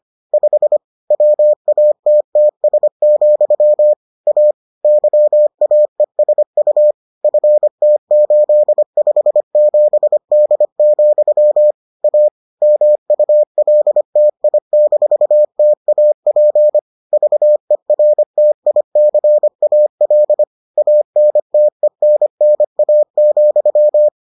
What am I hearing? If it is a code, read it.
Morse code, heard as a keyed tone, about 25 wpm, 600 Hz. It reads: 5 WATTS, A YAESU FT857D, A MULTI-TAP VERTICAL ANTENNA,